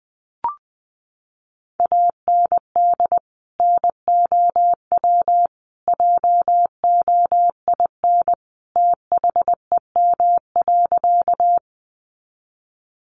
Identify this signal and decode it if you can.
Morse code, heard as a keyed tone, about 20 wpm, 700 Hz. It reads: AND NOW JOIN THEM.
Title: Sentences from Top 1400 Words 20wpm